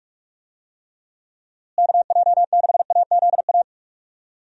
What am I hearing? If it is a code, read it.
Morse code, heard as a keyed tone, about 46 wpm, 700 Hz. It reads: KJ6AZA
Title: USA Call Signs - 6 Characters - Repeated 3x - 45wpm